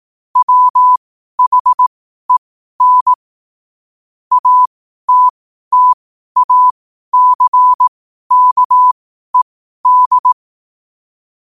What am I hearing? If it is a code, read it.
Morse code, heard as a keyed tone, about 18 wpm, spaced as for 12 wpm, 1000 Hz. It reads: WHEN ATTACKED
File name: qotdfiles / 12wpm